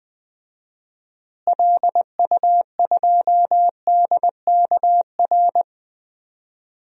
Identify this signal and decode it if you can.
Morse code, heard as a keyed tone, about 20 wpm, 700 Hz. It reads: LU2DKR
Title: DXCC-100 Call Signs - 6 Characters - Repeated 2x - 20wpm